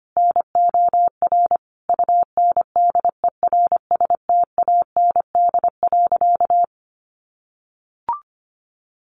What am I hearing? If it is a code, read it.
Morse code, heard as a keyed tone, about 25 wpm, 700 Hz. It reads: NOR UNDERSTAND.
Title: Sentences from Top 1500 Words 25wpm